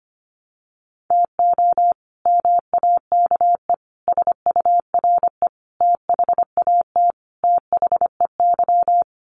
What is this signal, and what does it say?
Morse code, heard as a keyed tone, about 25 wpm, 700 Hz. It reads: TO MAKE SURE THAT THEY
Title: Sets of 5 Words Encore2 25wpm